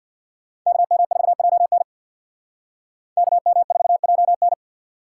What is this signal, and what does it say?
Morse code, heard as a keyed tone, about 64 wpm, 700 Hz. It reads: KM4JN KM4JN